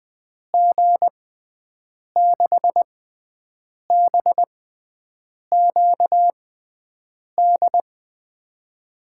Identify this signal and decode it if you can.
Morse code, heard as a keyed tone, about 20 wpm, spaced as for 7 wpm, 700 Hz. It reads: G6BQD
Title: USA Call Signs - 6x Character Spacing - 20wpm